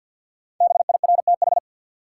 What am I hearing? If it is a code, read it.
Morse code, heard as a keyed tone, about 51 wpm, 700 Hz. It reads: BIRTH
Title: Sentences from Top 800 Words - New Word Drill - 50wpm